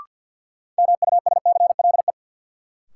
Morse code, heard as a keyed tone, about 50 wpm, 700 Hz. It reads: MUSCLE
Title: Sentences from Top 1300 Words - New Word Drill - 50wpm